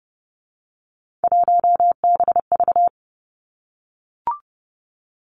Morse code, heard as a keyed tone, about 30 wpm, 700 Hz. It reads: 1BV
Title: Three Challenging Letter-Number Combinations 30wpm